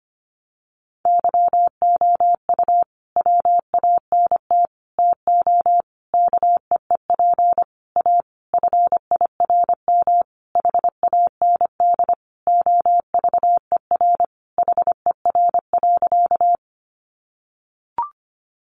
Morse code, heard as a keyed tone, about 25 wpm, 700 Hz. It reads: YOU WANT TO KEEP A FIRM HAND OVER HER.
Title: Sentences from Top 1400 Words 25wpm